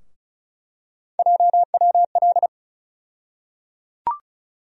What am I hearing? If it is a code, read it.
Morse code, heard as a keyed tone, about 36 wpm, 700 Hz. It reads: JWL
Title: USA Call Sign Postfixes - 3 Letters 35wpm